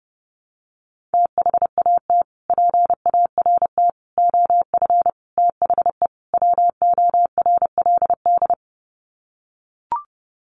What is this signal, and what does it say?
Morse code, heard as a keyed tone, about 29 wpm, 700 Hz. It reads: THAT PART OF THE WORLD
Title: Sets of 5 Words Encore2 30wpm